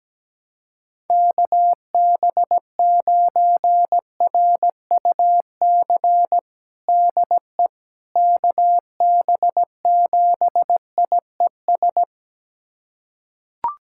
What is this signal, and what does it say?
Morse code, heard as a keyed tone, about 17 wpm, 700 Hz. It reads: KB9RUC DE KB7IES
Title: USA Callsign DE Callsign - 6 Characters - 17wpm